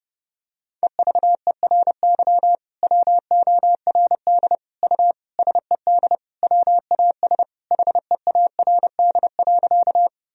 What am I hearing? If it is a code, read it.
Morse code, heard as a keyed tone, about 32 wpm, 700 Hz. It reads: EVERY WORD U SED WAS HEARD.
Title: Sentences from Top 400 Words and Abbreviations 30wpm